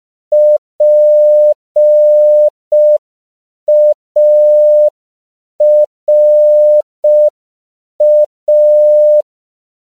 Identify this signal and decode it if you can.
Morse code, heard as a keyed tone, about 5 wpm, 600 Hz. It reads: PARA
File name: CW / 5PPM